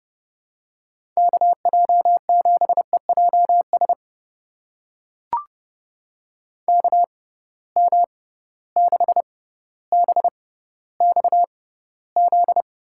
Morse code, heard as a keyed tone, about 30 wpm, spaced as for 15 wpm, 700 Hz. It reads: KJ7EJS KM6BXZ
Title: USA Call Signs - 6 Characters - 6x Character Spacing - 30wpm